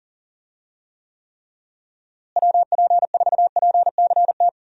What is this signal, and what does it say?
Morse code, heard as a keyed tone, about 40 wpm, 700 Hz. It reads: WP4PCT
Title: USA Call Signs - 6 Characters - Repeated 3x - 40wpm